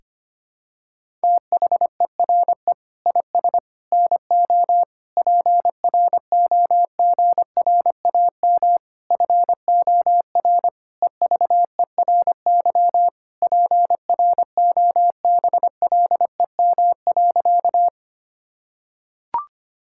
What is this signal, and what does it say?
Morse code, heard as a keyed tone, about 25 wpm, 700 Hz. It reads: THERE IS NO PROGRAM FOR EVERY PROBLEM.
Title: Sentences from Top 1000 Words - Review Part 5 - 25wpm